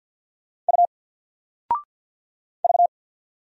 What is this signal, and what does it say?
Morse code, heard as a keyed tone, about 52 wpm, 700 Hz. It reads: U V
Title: Single Letter-Number Alt 50wpm